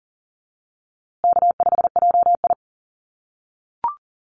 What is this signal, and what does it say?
Morse code, heard as a keyed tone, about 40 wpm, 700 Hz. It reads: K5JI
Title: USA Call Signs Repeated 3x 40wpm